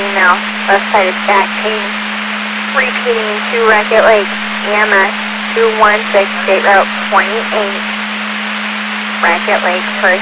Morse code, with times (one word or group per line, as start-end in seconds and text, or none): none